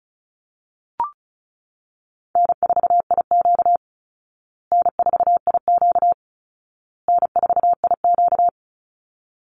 Morse code, heard as a keyed tone, about 34 wpm, 700 Hz. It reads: N4IQ N4IQ N4IQ
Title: Easy Call Signs Repeated 3x - 35wpm